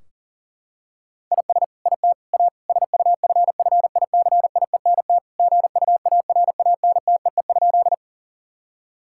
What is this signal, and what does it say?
Morse code, heard as a keyed tone, about 40 wpm, 700 Hz. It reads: IS IT A SUFFICIENT GUARANTEE?
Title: Sentences from Top 1600 Words 40wpm